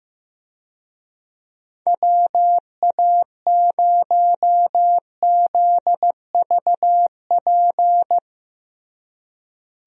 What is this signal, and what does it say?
Morse code, heard as a keyed tone, about 15 wpm, 700 Hz. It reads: WA0ZVP
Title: USA Call Signs - 6 Characters - Repeated 2x - 15wpm